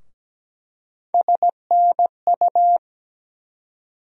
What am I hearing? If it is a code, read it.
Morse code, heard as a keyed tone, about 17 wpm, 700 Hz. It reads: SNU
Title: USA Call Sign Postfixes - 3 Letters 17wpm